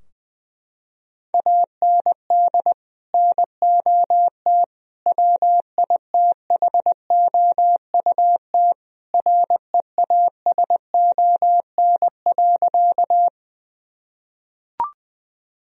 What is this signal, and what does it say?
Morse code, heard as a keyed tone, about 20 wpm, 700 Hz. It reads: AND NOT WITHOUT REASON.
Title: Sentences from Top 1000 Words - Review Part 3 - 20wpm